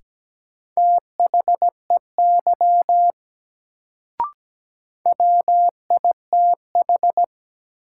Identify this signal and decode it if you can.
Morse code, heard as a keyed tone, about 17 wpm, 700 Hz. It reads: THEY WITH
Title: Sentences from Top 100 Words - New Word Drill - 17wpm